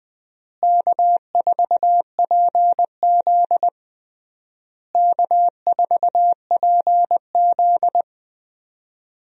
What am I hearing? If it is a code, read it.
Morse code, heard as a keyed tone, about 20 wpm, 700 Hz. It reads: K4PZ K4PZ
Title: USA Call Signs - 4 Characters - Repeated 3x - 20wpm